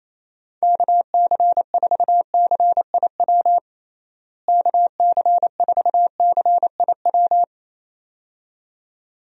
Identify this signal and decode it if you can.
Morse code, heard as a keyed tone, about 27 wpm, 700 Hz. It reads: KC4CIW KC4CIW
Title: USA Call Signs - 6 Characters - Repeated 2x - 28wpm